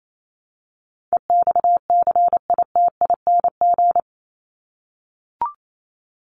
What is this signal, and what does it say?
Morse code, heard as a keyed tone, about 28 wpm, 700 Hz. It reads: EXCITING
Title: Top 1751-2000 Words 28wpm